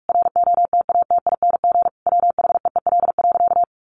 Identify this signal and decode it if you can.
Morse code, heard as a keyed tone, about 44 wpm, 700 Hz. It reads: ROTATING WHEEL.
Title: Additional Easy Phrases 45wpm